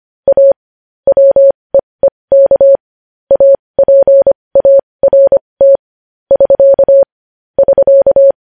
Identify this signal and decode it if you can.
Morse code, heard as a keyed tone, about 25 wpm, spaced as for 20 wpm, 550 Hz. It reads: A WEEK APART <SK> <SK>
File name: podcast / News headlines-1x-25x20